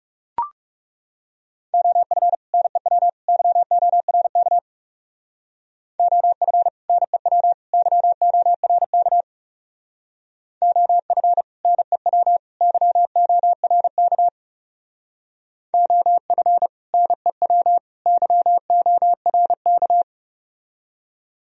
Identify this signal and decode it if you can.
Morse code, heard as a keyed tone, about 37 wpm, 700 Hz. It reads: OF NEW YORK OF NEW YORK OF NEW YORK OF NEW YORK
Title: Speed-Racing - Sets of 3 Words 30wpm